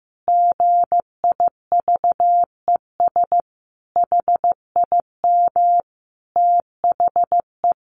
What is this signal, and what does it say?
Morse code, heard as a keyed tone, about 15 wpm, 700 Hz. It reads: GIVES HIM THE
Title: Additional Easy Phrases 15wpm